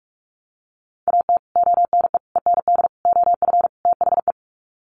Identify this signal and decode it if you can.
Morse code, heard as a keyed tone, about 44 wpm, 700 Hz. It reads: AT ONE END OF THE
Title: Sets of 5 Words Encore 45wpm